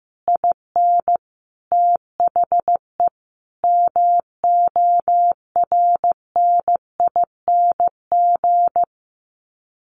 Morse code, heard as a keyed tone, about 15 wpm, 700 Hz. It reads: IN THE MORNING
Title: Sets of 5 Words Encore2 15wpm